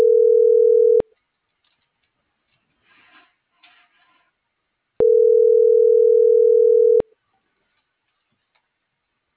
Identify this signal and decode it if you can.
Ringback tone (North America), 440 + 480 Hz.